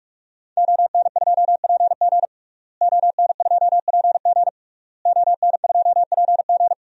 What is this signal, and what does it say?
Morse code, heard as a keyed tone, about 46 wpm, 700 Hz. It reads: ON2PG ON2PG ON2PG
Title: DXCC-100 Call Signs - 5 Characters - Repeated 3x - 45wpm